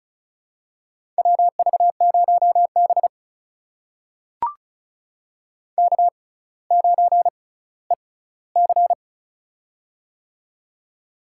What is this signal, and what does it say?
Morse code, heard as a keyed tone, about 34 wpm, spaced as for 18 wpm, 700 Hz. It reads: WV0B K9EC
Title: USA Call Signs - 4 Characters - 6x Character Spacing - 35wpm